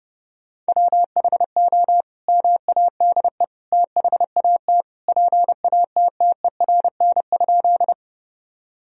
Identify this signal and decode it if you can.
Morse code, heard as a keyed tone, about 30 wpm, 700 Hz. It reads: WHO MADE THAT PATTERN?
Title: Sentences from Top 1000 Words - Review Part 2 - 30wpm